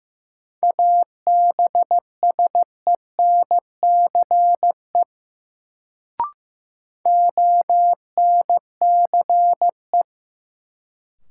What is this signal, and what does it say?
Morse code, heard as a keyed tone, about 15 wpm, 700 Hz. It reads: ABSENCE ONCE
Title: Sets of 4 Words - New Word Drill - 15wpm